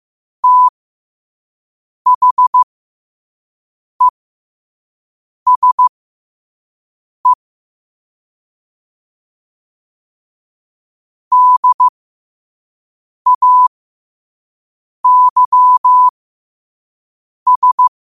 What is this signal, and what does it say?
Morse code, heard as a keyed tone, about 15 wpm, spaced as for 5 wpm, 1000 Hz. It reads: THESE DAYS
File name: qotdfiles / 5wpm